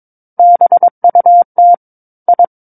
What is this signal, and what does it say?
Morse code, heard as a keyed tone, about 22 wpm, 700 Hz. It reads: BUT I
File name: Book One / 13-Chapter 13 - 22 WPM